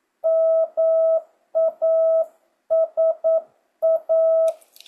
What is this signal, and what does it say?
Morse code, heard as a keyed tone, about 9 wpm, 650 Hz. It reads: MASA
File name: jh8cbh / 2025cw